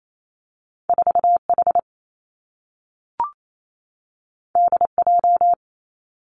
Morse code, heard as a keyed tone, about 28 wpm, 700 Hz. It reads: 4 H D J
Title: Two Challenging Letter-Number Combinations 28wpm